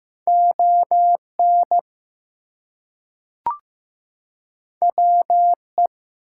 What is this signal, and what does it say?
Morse code, heard as a keyed tone, about 15 wpm, 700 Hz. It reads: ON WE
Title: Two Letter Words - 15wpm